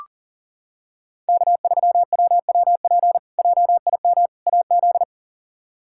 Morse code, heard as a keyed tone, about 40 wpm, 700 Hz. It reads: K3WWP JIM AZ
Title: K1USN SST QSO Practice - 40wpm